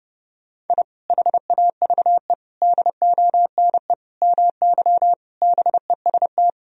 Morse code, heard as a keyed tone, about 32 wpm, 700 Hz. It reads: I HAVE DONE MY BEST
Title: Sentences from Top 1000 Words - Review Part 4 - 30wpm